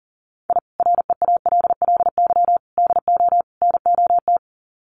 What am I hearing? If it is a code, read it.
Morse code, heard as a keyed tone, about 40 wpm, 700 Hz. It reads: I REALLY DO NOT
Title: Sets of 4 Words 40wpm